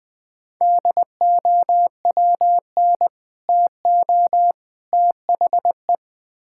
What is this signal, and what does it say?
Morse code, heard as a keyed tone, about 20 wpm, 700 Hz. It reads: DOWN TO THE
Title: Build A Phrase - 3-Word Phrases - 20wpm